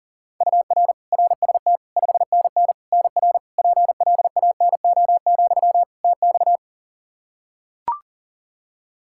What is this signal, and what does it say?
Morse code, heard as a keyed tone, about 40 wpm, 700 Hz. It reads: UR RST 5NN NR PLANO, TX
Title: Practice QSO - Exchange 1 - Beginning - 40wpm